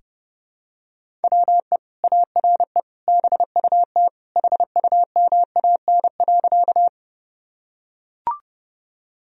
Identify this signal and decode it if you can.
Morse code, heard as a keyed tone, about 30 wpm, 700 Hz. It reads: WE ARE BUT HUMAN.
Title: Sentences from Top 1000 Words - Review Part 3 - 30wpm